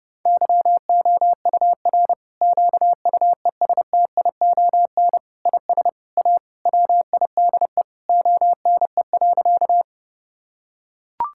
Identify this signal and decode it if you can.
Morse code, heard as a keyed tone, about 30 wpm, 700 Hz. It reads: YOUR QUESTION IS A WIDE ONE.
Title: Sentences from Top 1000 Words - Review Part 6 - 30wpm